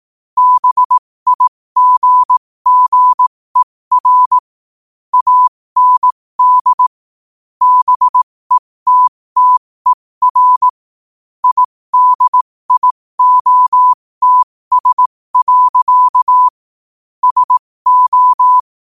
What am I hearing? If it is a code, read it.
Morse code, heard as a keyed tone, about 18 wpm, spaced as for 15 wpm, 1000 Hz. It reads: BIGGER AND BETTER IDIOTS. SO
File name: qotdfiles / 15wpm